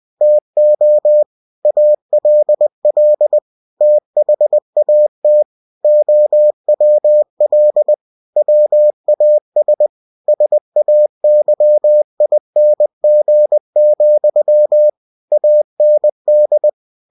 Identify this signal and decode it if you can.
Morse code, heard as a keyed tone, about 20 wpm, 600 Hz. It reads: TO ALL THAT OWL WAS SAYING, AND
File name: cw-books / winnie-the-pooh / winnie-20wpm-page-0031